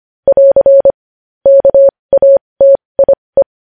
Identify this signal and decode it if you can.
Morse code, heard as a keyed tone, about 25 wpm, spaced as for 20 wpm, 550 Hz. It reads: <AR> KATIE
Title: News Headlines at 20